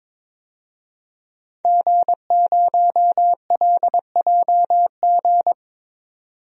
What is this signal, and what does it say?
Morse code, heard as a keyed tone, about 22 wpm, 700 Hz. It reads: G0LJG
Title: DXCC-100 Call Signs - 5 Characters - 22wpm